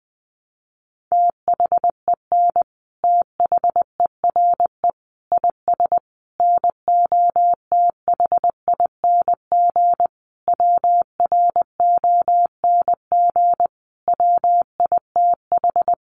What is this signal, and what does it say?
Morse code, heard as a keyed tone, about 20 wpm, 700 Hz. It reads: THEN THERE IS NOTHING WRONG WITH
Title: Sentences from Top 900 Words 20wpm